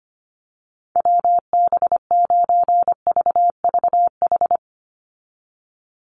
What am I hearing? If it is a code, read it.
Morse code, heard as a keyed tone, about 25 wpm, 690 Hz. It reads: WB9VVH
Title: USA Call Signs 25wpm